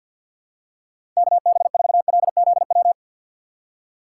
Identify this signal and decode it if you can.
Morse code, heard as a keyed tone, about 50 wpm, 700 Hz. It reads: KB4LZW
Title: USA Call Signs - 6 Characters - 6x Character Spacing - 50wpm